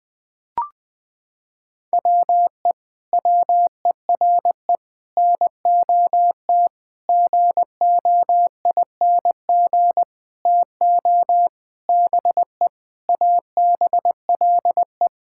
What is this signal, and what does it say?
Morse code, heard as a keyed tone, about 20 wpm, 700 Hz. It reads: WE WERE NOT GOING TO BE ABLE
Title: Sentences with 7-Word Phrases - 20wpm